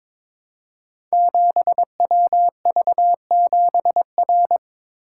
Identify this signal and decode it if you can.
Morse code, heard as a keyed tone, about 22 wpm, 700 Hz. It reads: 7WV7R
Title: Code Groups - 5 Letter:Numbers - 22wpm